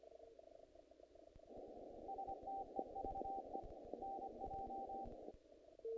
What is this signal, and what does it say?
Morse code, heard as a keyed tone, about 25 wpm, 780 Hz, with noise. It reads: STEVE NJ
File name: contests / naqp-cw-k5tr-op-k5pi- -1800